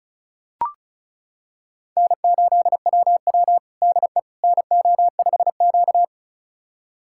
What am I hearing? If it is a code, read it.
Morse code, heard as a keyed tone, about 36 wpm, 700 Hz. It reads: N8WW DE NO5Q
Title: USA Callsign DE Callsign - 4 Characters - 35wpm